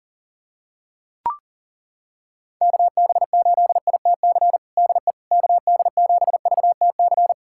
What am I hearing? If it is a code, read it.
Morse code, heard as a keyed tone, about 37 wpm, 700 Hz. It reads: KB8ITC DE KD7VTC
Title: USA Callsign DE Callsign - 6 Characters - 40wpm